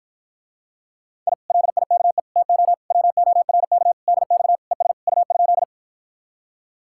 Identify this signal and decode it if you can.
Morse code, heard as a keyed tone, about 54 wpm, 700 Hz. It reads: I LIKE TO WORK DX ES U?